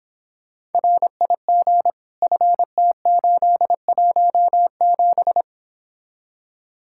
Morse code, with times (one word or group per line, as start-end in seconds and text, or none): 0.7-1.9 RIG
2.2-5.4 FT817